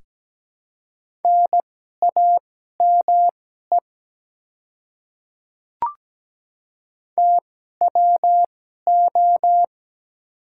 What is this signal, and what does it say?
Morse code, heard as a keyed tone, about 17 wpm, spaced as for 12 wpm, 700 Hz. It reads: NAME TWO
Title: 2x Character Spacing - Top 200 Words - 17wpm